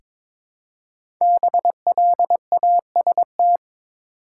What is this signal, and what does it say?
Morse code, heard as a keyed tone, about 22 wpm, 700 Hz. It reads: BLAST
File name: 5 Letter Words - First 3 Letters Given - 22wpm